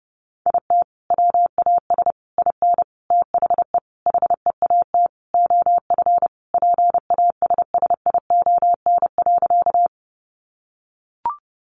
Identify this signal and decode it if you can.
Morse code, heard as a keyed tone, about 30 wpm, 700 Hz. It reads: IT WAS IN THE HEAT OF PASSION.
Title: Sentences from Top 1300 Words 30wpm